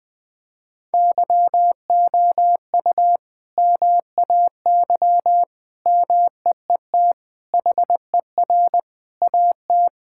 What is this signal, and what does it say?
Morse code, heard as a keyed tone, about 20 wpm, 700 Hz. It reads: YOU MAY MEET HER AT
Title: Sentences from Top 1600 Words 20wpm